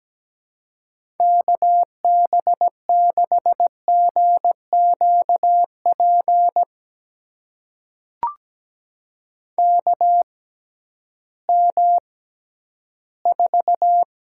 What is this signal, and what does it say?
Morse code, heard as a keyed tone, about 17 wpm, spaced as for 11 wpm, 700 Hz. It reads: KB6GQP KM4
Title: USA Call Signs - 6 Characters - 6x Character Spacing - 17wpm